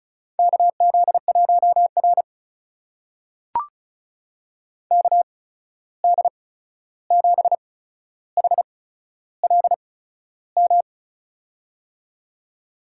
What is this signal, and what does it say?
Morse code, heard as a keyed tone, about 36 wpm, spaced as for 13 wpm, 700 Hz. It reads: KZ1R KD7HLM
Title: USA Call Signs - 8x Character Spacing - 35wpm